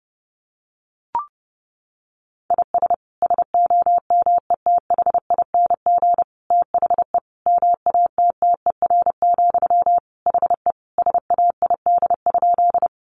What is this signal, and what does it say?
Morse code, heard as a keyed tone, about 30 wpm, 700 Hz. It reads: IS SOMETHING THE MATTER, HE SAID?
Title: Sentences from Top 700 Words 30wpm